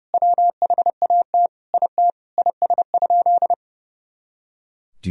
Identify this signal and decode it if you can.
Morse code, heard as a keyed tone, about 32 wpm, 700 Hz. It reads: WHAT IT IS?
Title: Sentences from Top 100 Words and Abbreviations 30wpm